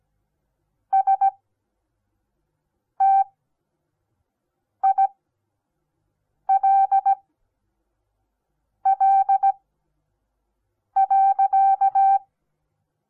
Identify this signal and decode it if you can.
Morse code, heard as a keyed tone, about 17 wpm, spaced as for 5 wpm, 780 Hz. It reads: STILL.
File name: Onlinelehrgang / morsen / DJ4UF Morsen08